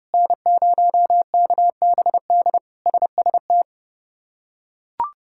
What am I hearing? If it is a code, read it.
Morse code, heard as a keyed tone, about 30 wpm, 700 Hz. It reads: N0KBD SST
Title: K1USN SST QSO Practice - 30wpm